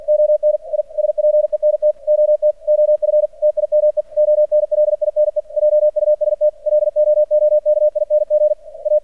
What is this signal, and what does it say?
Morse code, heard as a keyed tone, about 49 wpm, 600 Hz. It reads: ON I SPENT OT OF TIME ONLINE JUST LOOKING A
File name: archived / CRSnet